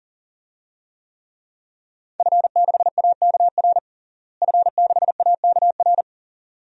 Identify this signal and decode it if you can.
Morse code, heard as a keyed tone, about 40 wpm, 700 Hz. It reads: F6AKR F6AKR
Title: DXCC-100 Call Signs - 5 Characters - Repeated 2x - 40wpm